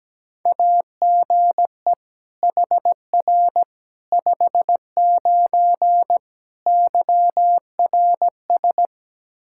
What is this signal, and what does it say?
Morse code, heard as a keyed tone, about 17 wpm, 700 Hz. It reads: AGE HR 59 YRS
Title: Practice QSO - Exchange 3 - More Info - 17wpm